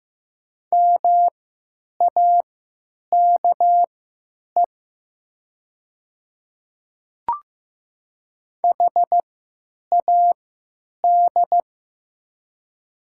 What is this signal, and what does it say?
Morse code, heard as a keyed tone, about 15 wpm, spaced as for 8 wpm, 700 Hz. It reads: MAKE HAD
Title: 3x Character Spacing - Top 300 Words - 15wpm